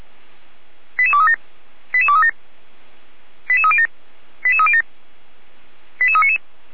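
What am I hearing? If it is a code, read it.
Five-tone selective call (ZVEI), heard as quick runs of five tones, tones of about 70 ms. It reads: 88237 88237 88307 88307 88399
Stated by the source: ZVEI five-tone call 88399 received via SDR, Feuerwehr Orlamünde.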